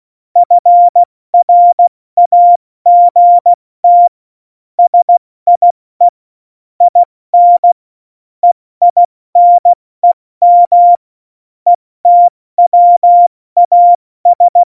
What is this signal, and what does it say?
Morse code, heard as a keyed tone, about 16 wpm, spaced as for 14 wpm, 700 Hz. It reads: FRAGT SIE IN EINEM ETWAS
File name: Morsekurs / Witze / Witz 11 Juli 17, Tempo 70 F80